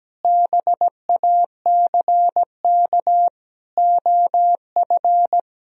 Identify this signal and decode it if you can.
Morse code, heard as a keyed tone, about 17 wpm, 700 Hz. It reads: BACK OF
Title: Sentences from Top 1000 Words - Review Part 1 - 17wpm